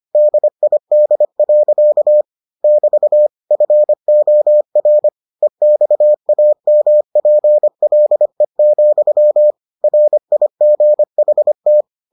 Morse code, heard as a keyed tone, about 25 wpm, 600 Hz. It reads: DID. <BT> FOR EXAMPLE, RIGHT